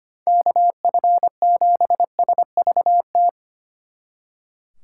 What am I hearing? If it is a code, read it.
Morse code, heard as a keyed tone, about 29 wpm, 700 Hz. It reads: KF7SVT
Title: Speed Racing - USA 2x3 Call Signs 25wpm